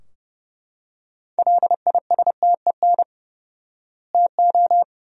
Morse code, heard as a keyed tone, about 29 wpm, 700 Hz. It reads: LISTEN TO
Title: 4x Word-Spacing - Sentences from Top 400 Words - 30wpm